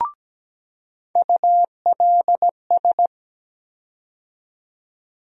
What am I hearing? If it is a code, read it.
Morse code, heard as a keyed tone, about 17 wpm, 700 Hz. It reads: ULS
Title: USA Call Sign Postfixes - 3 Letters 17wpm